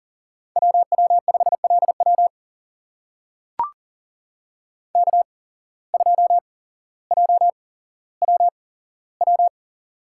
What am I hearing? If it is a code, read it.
Morse code, heard as a keyed tone, about 40 wpm, spaced as for 17 wpm, 700 Hz. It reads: WW5LW K2JWW
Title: USA Call Signs - 5 Characters - 8x Character Spacing - 40wpm